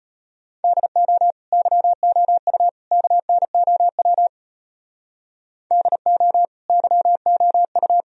Morse code, heard as a keyed tone, about 31 wpm, 700 Hz. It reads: DO YOU KNOW DO YOU
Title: Speed-Racing - Sets of 3 Words 25wpm